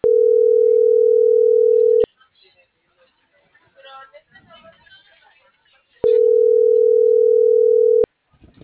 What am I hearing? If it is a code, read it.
Ringback tone (North America), 440 + 480 Hz.